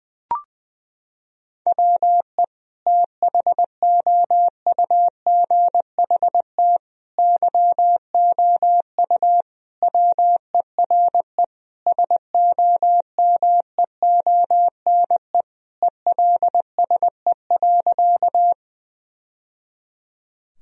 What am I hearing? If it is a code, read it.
Morse code, heard as a keyed tone, about 20 wpm, 700 Hz. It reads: WE THOUGHT YOU WERE SOMEONE ELSE.
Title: Sentences from Top 1000 Words - Review Part 4 - 20wpm